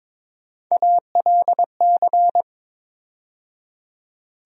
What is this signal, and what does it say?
Morse code, heard as a keyed tone, about 22 wpm, 700 Hz. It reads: ALC
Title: USA Call Sign Postfixes - 3 Letters 22wpm